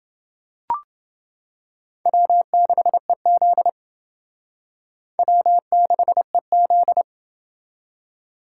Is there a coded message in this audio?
Yes: Morse code from 2.1 to 7.0 s, heard as a keyed tone, about 29 wpm, 700 Hz. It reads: W6EZ W6EZ